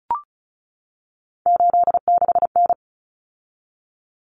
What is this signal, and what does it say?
Morse code, heard as a keyed tone, about 34 wpm, 700 Hz. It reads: 86N
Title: Common Cut Numbers - 3-Digit 35wpm